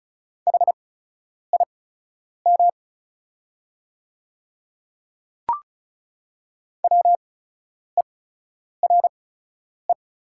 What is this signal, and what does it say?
Morse code, heard as a keyed tone, about 36 wpm, spaced as for 10 wpm, 700 Hz. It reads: HIM WERE
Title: 8x Character Spacing - Top 100 Words - 35wpm